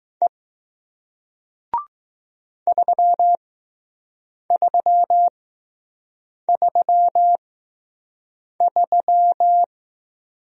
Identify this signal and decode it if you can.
Morse code, heard as a keyed tone, about 20 wpm, 700 Hz. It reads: E 3 3 3 3